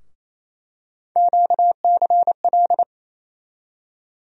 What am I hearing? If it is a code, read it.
Morse code, heard as a keyed tone, about 28 wpm, 700 Hz. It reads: QCL
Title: USA Call Sign Postfixes - 3 Letters 28wpm